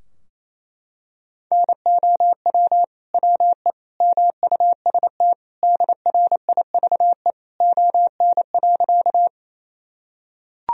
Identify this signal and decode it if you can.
Morse code, heard as a keyed tone, about 28 wpm, 700 Hz. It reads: NOW WE MUST DRIVE ON.
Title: Sentences from Top 1000 Words - Review Part 2 - 28wpm